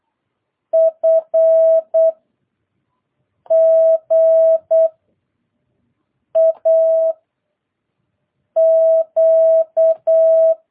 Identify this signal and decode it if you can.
Morse code, heard as a keyed tone, about 8 wpm, 660 Hz. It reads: F G A Q